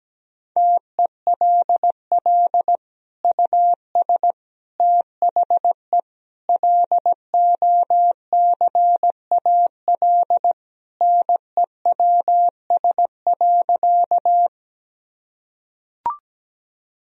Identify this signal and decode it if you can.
Morse code, heard as a keyed tone, about 17 wpm, 700 Hz. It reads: TELL US THE LOCAL NEWS.
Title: Sentences from Top 1000 Words - Review Part 3 - 17wpm